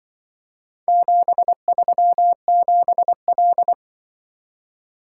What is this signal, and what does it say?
Morse code, heard as a keyed tone, about 24 wpm, 700 Hz. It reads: 737L
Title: Speed-Racing - Four Letter-Number Combinations - 20wpm